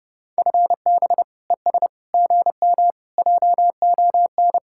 Morse code, heard as a keyed tone, about 30 wpm, 700 Hz. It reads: FB ES GM JON